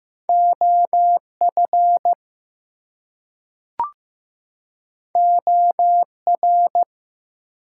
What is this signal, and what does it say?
Morse code, heard as a keyed tone, about 15 wpm, 700 Hz. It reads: OF OR
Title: Two Letter Words - 15wpm